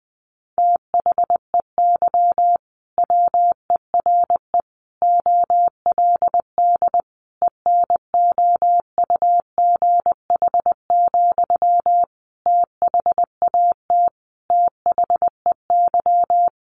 Morse code, heard as a keyed tone, about 19 wpm, 700 Hz. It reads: THEY WERE OLD ENOUGH, THAT THEY
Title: Sentences from Top 800 Words 20wpm